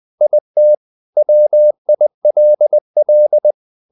Morse code, heard as a keyed tone, about 20 wpm, 600 Hz. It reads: IT WILL
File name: the old man and the sea part 2 20wpm